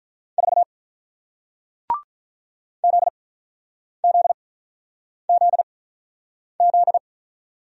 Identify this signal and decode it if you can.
Morse code, heard as a keyed tone, about 47 wpm, spaced as for 9 wpm, 700 Hz. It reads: 4 ZZZZ